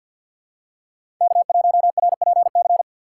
Morse code, heard as a keyed tone, about 52 wpm, 700 Hz. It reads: K1RPC
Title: USA Call Signs - 5 Characters - 3x Character Spacing - 50wpm